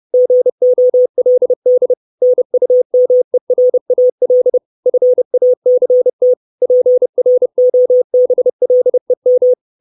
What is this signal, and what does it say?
Morse code, heard as a keyed tone, about 30 wpm, 500 Hz. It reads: GOLD NUMERAL FACT PROBLEM
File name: 632 Words 30WPM